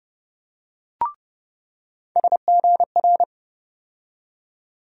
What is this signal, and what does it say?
Morse code, heard as a keyed tone, about 30 wpm, 700 Hz. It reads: SGR
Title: USA Call Sign Postfixes - 3 Letters 30wpm